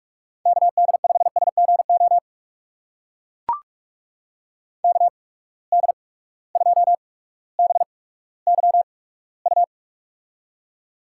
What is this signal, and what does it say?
Morse code, heard as a keyed tone, about 45 wpm, spaced as for 19 wpm, 700 Hz. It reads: KD5SGO KD2BYU
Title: USA Call Signs - 6 Characters - 8x Character Spacing - 45wpm